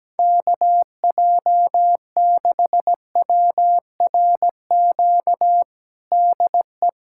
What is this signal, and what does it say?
Morse code, heard as a keyed tone, about 17 wpm, 700 Hz. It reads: KJ6WRQ DE
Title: USA Callsign DE Callsign - 6 Characters - 17wpm